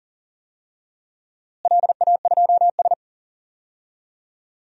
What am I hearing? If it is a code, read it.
Morse code, heard as a keyed tone, about 37 wpm, 700 Hz. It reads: LA2S
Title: DXCC-100 Call Signs - 4 Characters - Repeated 3x - 40wpm